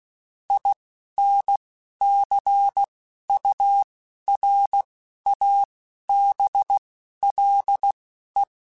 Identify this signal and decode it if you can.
Morse code, heard as a keyed tone, about 16 wpm, spaced as for 11 wpm, 800 Hz. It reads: INCURABLE
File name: Quotes / Quotes-mp3s / Optimist @10